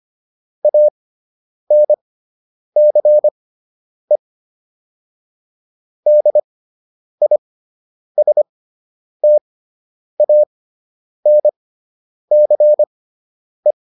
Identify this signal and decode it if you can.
Morse code, heard as a keyed tone, about 25 wpm, spaced as for 9 wpm, 600 Hz. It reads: ANCE DISTANCE